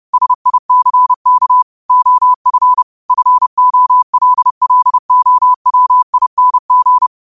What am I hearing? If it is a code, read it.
Morse code, heard as a keyed tone, about 30 wpm, 1000 Hz. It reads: SICK OF FOLLOWING
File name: qotdfiles / 30wpm